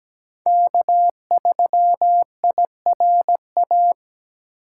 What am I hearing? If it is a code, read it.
Morse code, heard as a keyed tone, about 17 wpm, 700 Hz. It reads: K3IRA
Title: USA Call Signs - 5 Characters - 3x Character Spacing - 17wpm